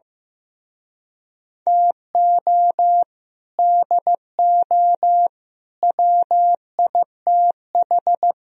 Morse code, heard as a keyed tone, about 15 wpm, 700 Hz. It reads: TO DO WITH
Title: Speed-Racing - Sets of 4 Words 15wpm